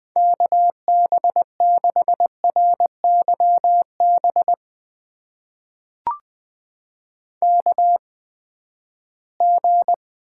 Morse code, heard as a keyed tone, about 21 wpm, spaced as for 14 wpm, 700 Hz. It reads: KB6RYB KG
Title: USA Call Signs - 6 Characters - 8x Character Spacing - 20wpm